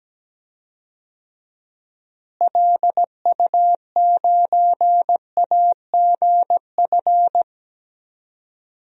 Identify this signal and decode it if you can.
Morse code, heard as a keyed tone, about 17 wpm, 700 Hz. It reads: LU9AGF
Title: DXCC-100 Call Signs - 6 Characters - Repeated 2x - 17wpm